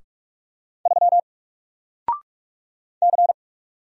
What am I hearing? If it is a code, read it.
Morse code, heard as a keyed tone, about 46 wpm, 700 Hz. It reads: 3 C
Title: Single Letter-Number Alt 45wpm